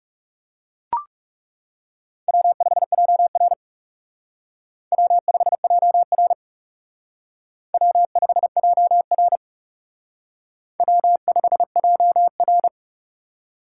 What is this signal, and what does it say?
Morse code, heard as a keyed tone, about 37 wpm, 700 Hz. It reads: W5JR W5JR W5JR W5JR
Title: Speed Racing - USA Call Signs 30wpm